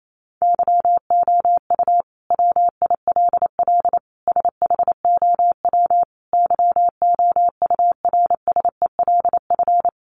Morse code, heard as a keyed tone, about 28 wpm, 700 Hz. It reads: YOU WILL SHOW YOURSELF
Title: Sentences from Top 600 Words 28wpm